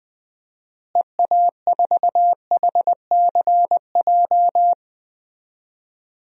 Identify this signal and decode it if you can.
Morse code, heard as a keyed tone, about 21 wpm, 700 Hz. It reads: EA4HCJ
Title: DXCC-100 Call Signs - 6 Characters - Repeated 2x - 20wpm